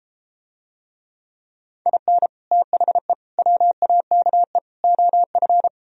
Morse code, heard as a keyed tone, about 33 wpm, 700 Hz. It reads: IN THE WAKE OF